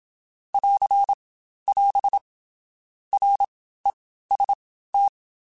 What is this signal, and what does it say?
Morse code, heard as a keyed tone, about 27 wpm, spaced as for 14 wpm, 800 Hz. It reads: <AR><AS> REST
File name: Basic Sets / Basic Letter Groups Collective7 / AWUNDG@25